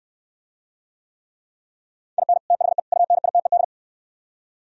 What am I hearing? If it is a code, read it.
Morse code, heard as a keyed tone, about 67 wpm, 700 Hz. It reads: IN THE UNITED